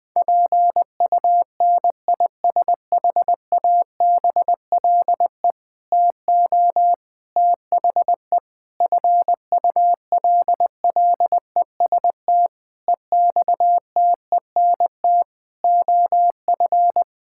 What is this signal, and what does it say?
Morse code, heard as a keyed tone, about 19 wpm, 700 Hz. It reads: PUNISHABLE TO THE FULLEST EXTENT OF
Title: Sentences with 7-Word Phrases - Spoken Beginning - 20wpm